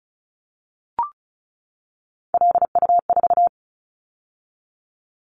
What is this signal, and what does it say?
Morse code, heard as a keyed tone, about 36 wpm, 700 Hz. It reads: LU4
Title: Callsign Prefixes for Top 100 DXCC - 35wpm